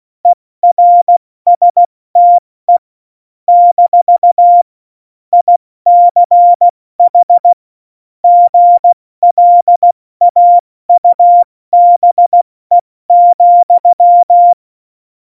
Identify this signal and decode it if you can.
Morse code, heard as a keyed tone, about 16 wpm, 700 Hz. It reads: ERSTE - ICH GLAUBE,
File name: Morsekurs / Witze / Witz 01. September 2020, 70 F80